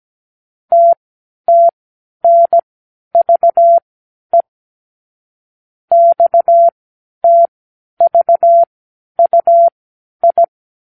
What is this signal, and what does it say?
Morse code, heard as a keyed tone, about 17 wpm, spaced as for 10 wpm, 680 Hz. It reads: TTNVE XTVUI
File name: k7qo / Lesson 36 (A-X)